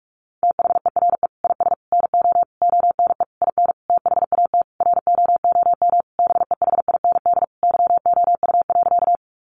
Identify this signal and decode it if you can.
Morse code, heard as a keyed tone, about 44 wpm, 700 Hz. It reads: THERE IS NO ONE IN THAT ROOM BEHIND YOU.
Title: Sentences from Top 500 Words 45wpm